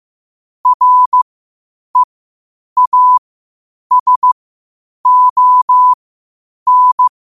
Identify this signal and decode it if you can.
Morse code, heard as a keyed tone, about 15 wpm, spaced as for 8 wpm, 1000 Hz. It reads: REASON
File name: qotdfiles / 8wpm